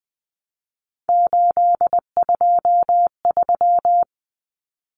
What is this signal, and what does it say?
Morse code, heard as a keyed tone, about 20 wpm, 700 Hz. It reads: 823
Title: Common Cut Numbers - 3-Digit 20wpm